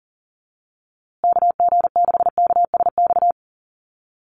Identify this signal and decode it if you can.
Morse code, heard as a keyed tone, about 40 wpm, 700 Hz. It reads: KG6KSX
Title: USA Call Signs Repeated 3x 40wpm